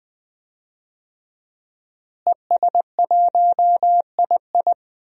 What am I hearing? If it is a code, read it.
Morse code, heard as a keyed tone, about 20 wpm, 700 Hz. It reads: ES1II
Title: DXCC-100 Call Signs - 5 Characters - Repeated 3x - 20wpm